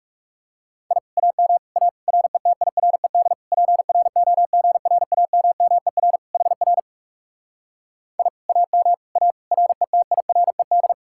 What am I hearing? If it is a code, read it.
Morse code, heard as a keyed tone, about 40 wpm, 700 Hz. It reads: I AM A RETIRED PROGRAMMER HR I AM A RETIRED